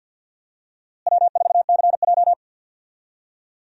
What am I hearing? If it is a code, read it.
Morse code, heard as a keyed tone, about 50 wpm, 700 Hz. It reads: W4CJ
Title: USA Call Signs - 4 Characters - Repeated 2x - 50wpm